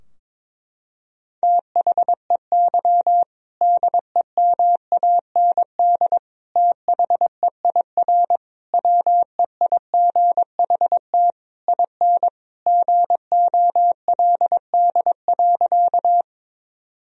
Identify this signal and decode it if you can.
Morse code, heard as a keyed tone, about 21 wpm, 700 Hz. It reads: THEY DEMAND THEIR WEIGHT IN GOLD.
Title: Sentences from Top 1000 Words - Review Part 3 - 22wpm